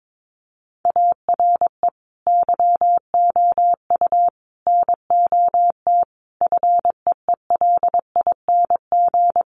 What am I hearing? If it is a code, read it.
Morse code, heard as a keyed tone, about 22 wpm, 700 Hz. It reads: ARE YOU NOT FEELING
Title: Sentences from Top 1200 Words 22wpm